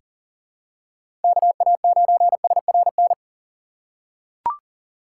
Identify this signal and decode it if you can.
Morse code, heard as a keyed tone, about 40 wpm, 700 Hz. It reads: KA9SRN
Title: USA Call Signs - 6 Characters - 3x Character Spacing - 40wpm